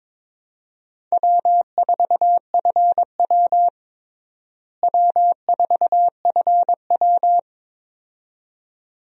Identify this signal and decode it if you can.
Morse code, heard as a keyed tone, about 21 wpm, 700 Hz. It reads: W4FW W4FW
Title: USA Call Signs - 4 Characters - Repeated 2x - 22wpm